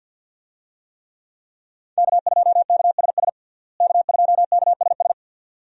Morse code, heard as a keyed tone, about 50 wpm, 700 Hz. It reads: K2KSS K2KSS
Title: USA Call Signs - 5 Characters - Repeated 2x - 50wpm